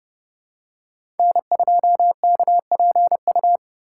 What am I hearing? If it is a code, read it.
Morse code, heard as a keyed tone, about 30 wpm, 700 Hz. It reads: N2KPU
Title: USA Call Signs - 5 Characters - 2x Character Spacing - 30wpm